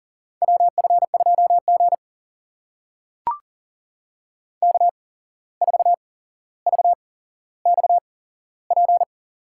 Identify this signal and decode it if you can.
Morse code, heard as a keyed tone, about 42 wpm, spaced as for 16 wpm, 700 Hz. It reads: WF2G K4VXP
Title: USA Call Signs - 8x Character Spacing - 40wpm